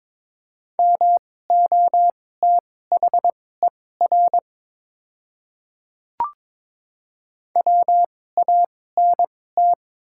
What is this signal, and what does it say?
Morse code, heard as a keyed tone, about 22 wpm, spaced as for 16 wpm, 690 Hz. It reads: MOTHER WANT
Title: 2x Character Spacing - Top 500 Words - 22wpm